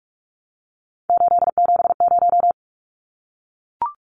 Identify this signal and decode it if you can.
Morse code, heard as a keyed tone, about 44 wpm, 700 Hz. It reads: 870
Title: Numbers - 3-Digit 45wpm